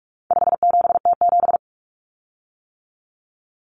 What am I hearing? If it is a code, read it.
Morse code, heard as a keyed tone, about 44 wpm, 700 Hz. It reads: 57T7
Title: Common Cut Numbers - 4-Digit 45wpm